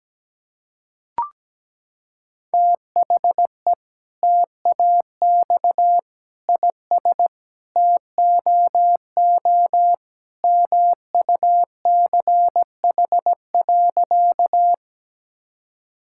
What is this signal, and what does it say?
Morse code, heard as a keyed tone, about 17 wpm, 700 Hz. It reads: THE TAX IS TOO MUCH.
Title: Sentences from Top 1000 Words - Review Part 5 - 17wpm